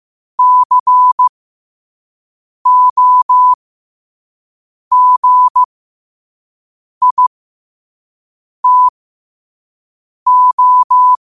Morse code, heard as a keyed tone, about 15 wpm, spaced as for 5 wpm, 1000 Hz. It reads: COGITO